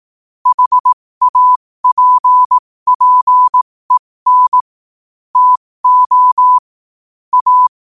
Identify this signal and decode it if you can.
Morse code, heard as a keyed tone, about 18 wpm, spaced as for 15 wpm, 1000 Hz. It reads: HAPPEN TO A
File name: qotdfiles / 15wpm